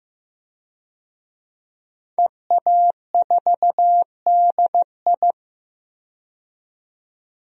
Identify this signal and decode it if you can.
Morse code, heard as a keyed tone, about 15 wpm, 700 Hz. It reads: EA4DI
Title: DXCC-100 Call Signs - 5 Characters - 15wpm